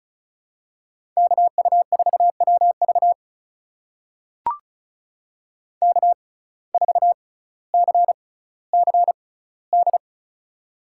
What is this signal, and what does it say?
Morse code, heard as a keyed tone, about 36 wpm, spaced as for 18 wpm, 700 Hz. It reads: KU4WV K4CCD
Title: USA Call Signs - 5 Characters - 6x Character Spacing - 35wpm